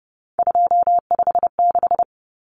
Morse code, heard as a keyed tone, about 30 wpm, 700 Hz. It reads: 256
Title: Common Cut Numbers - 3-Digit 30wpm